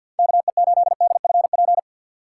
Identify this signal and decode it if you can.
Morse code, heard as a keyed tone, about 52 wpm, 700 Hz. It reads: KE8DFP
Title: USA Call Signs - 4x Character Spacing - 50wpm